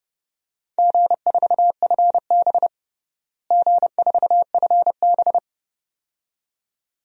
Morse code, heard as a keyed tone, about 32 wpm, 700 Hz. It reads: G4FB G4FB
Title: DXCC-100 Call Signs - 4 Characters - Repeated 2x - 30wpm